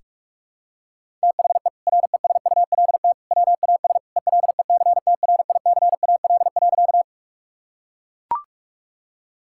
Morse code, heard as a keyed tone, about 41 wpm, 700 Hz. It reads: THE RESULT WAS ELECTRICAL.
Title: Sentences from Top 1000 Words - Review Part 6 - 45wpm